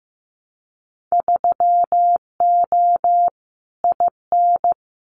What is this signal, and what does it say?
Morse code, heard as a keyed tone, about 15 wpm, 700 Hz. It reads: 3O IN
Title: Winter Field Day Exchanges Encore 15wpm